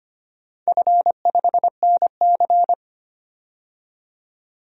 Morse code, heard as a keyed tone, about 25 wpm, 700 Hz. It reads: F5NC
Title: DXCC-100 Call Signs - 4 Characters - 25wpm